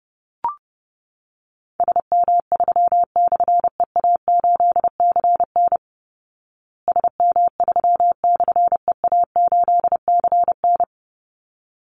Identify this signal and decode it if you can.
Morse code, heard as a keyed tone, about 30 wpm, 700 Hz. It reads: SM3/EA8CN SM3/EA8CN
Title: Call Signs Repeated 3x - 30wpm